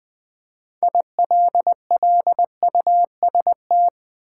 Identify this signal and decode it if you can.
Morse code, heard as a keyed tone, about 20 wpm, 700 Hz. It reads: ILLUST
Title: Build A Word - 6 Letters At A Time - 12-Letter Words - 20wpm